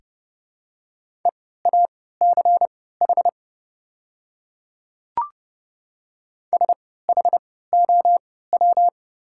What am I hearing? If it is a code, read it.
Morse code, heard as a keyed tone, about 30 wpm, spaced as for 17 wpm, 700 Hz. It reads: EACH SHOW
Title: 3x Character Spacing - Top 200 Words - 30wpm